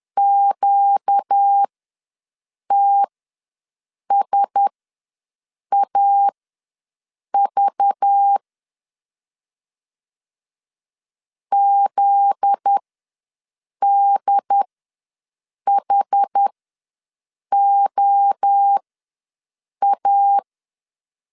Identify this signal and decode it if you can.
Morse code, heard as a keyed tone, about 10 wpm, spaced as for 6 wpm, 800 Hz. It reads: QTSAV ZDHOA
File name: LEZIONI / 7 GF9Z / 7 GF9Z ws 10wpm 3919